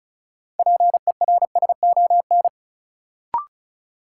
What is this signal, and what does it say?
Morse code, heard as a keyed tone, about 34 wpm, 700 Hz. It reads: PERSON
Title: Sentences from Top 400 Words - New Word Drill - 35wpm